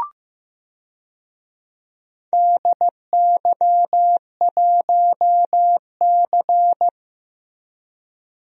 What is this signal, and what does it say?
Morse code, heard as a keyed tone, about 15 wpm, 700 Hz. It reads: DY1C
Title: DXCC-100 Call Signs - 4 Characters - 15wpm